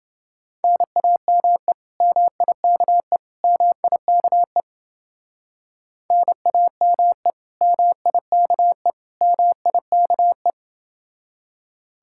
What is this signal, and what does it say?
Morse code, heard as a keyed tone, about 29 wpm, 700 Hz. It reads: NAME MIKE MIKE NAME MIKE MIKE